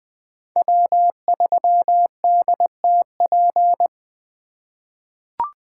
Morse code, heard as a keyed tone, about 20 wpm, 700 Hz. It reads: W3DTP
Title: USA Call Signs - 5 Characters - 3x Character Spacing - 20wpm